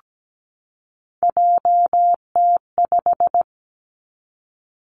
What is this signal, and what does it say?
Morse code, heard as a keyed tone, about 17 wpm, 700 Hz. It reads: JT5
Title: Three Challenging Letter-Number Combinations 17wpm